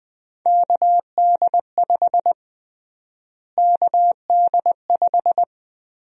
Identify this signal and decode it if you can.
Morse code, heard as a keyed tone, about 20 wpm, 700 Hz. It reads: KD5 KD5
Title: USA Call Sign Prefixes - Repeated 2x - 20wpm